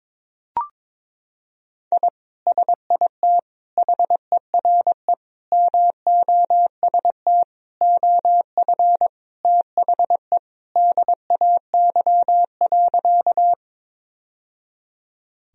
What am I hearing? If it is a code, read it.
Morse code, heard as a keyed tone, about 22 wpm, 700 Hz. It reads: I SIT HERE MOST OF THE DAY.
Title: Sentences from Top 1000 Words - Review Part 1 - 22wpm